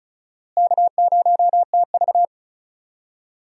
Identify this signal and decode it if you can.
Morse code, heard as a keyed tone, about 36 wpm, 700 Hz. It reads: K0TV
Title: USA Call Signs - 4 Characters - Repeated 3x - 35wpm